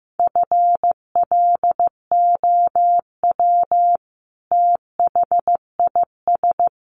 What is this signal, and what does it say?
Morse code, heard as a keyed tone, about 15 wpm, 700 Hz. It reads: FLOW THIS
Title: Sentences from Top 1200 Words 15wpm